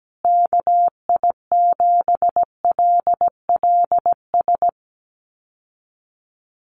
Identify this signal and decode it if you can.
Morse code, heard as a keyed tone, about 17 wpm, 700 Hz. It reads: KI7LLS
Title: USA Call Signs Repeated 3x 17wpm